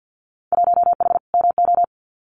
Morse code, heard as a keyed tone, about 50 wpm, 700 Hz. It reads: 1H MO
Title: Winter Field Day Exchanges 50wpm